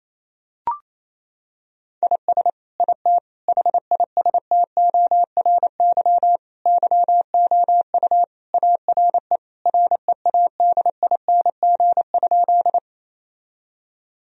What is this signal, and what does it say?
Morse code, heard as a keyed tone, about 27 wpm, 700 Hz. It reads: IS IT HISTORY YOU ARE READING?
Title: Sentences from Top 1000 Words - Review Part 3 - 28wpm